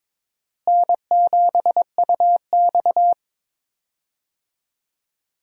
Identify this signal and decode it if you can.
Morse code, heard as a keyed tone, about 22 wpm, 700 Hz. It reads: N7UX
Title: USA Call Signs - 4 Characters - Repeated 2x - 22wpm